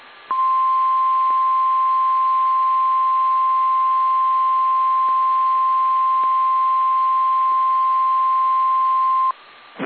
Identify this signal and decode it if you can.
Emergency Alert System (SAME) tones (the 1050 Hz attention tone). It is heard from 0.3 to 9.3 s.
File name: eas / KSDZKDJL / KSDZKDJL